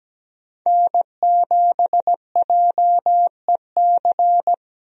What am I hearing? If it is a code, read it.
Morse code, heard as a keyed tone, about 17 wpm, 700 Hz. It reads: N7JEC
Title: USA Call Signs - 5 Characters - 17wpm